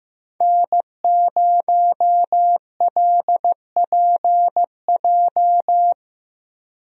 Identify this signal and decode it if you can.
Morse code, heard as a keyed tone, about 15 wpm, 700 Hz. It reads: N0LPJ
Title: USA Call Signs - 5 Characters - Repeated 2x - 15wpm